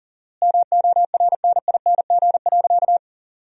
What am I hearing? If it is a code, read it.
Morse code, heard as a keyed tone, about 37 wpm, 700 Hz. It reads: MORNING.
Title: 4x Word-Spacing - Sentences from Top 400 Words - 40wpm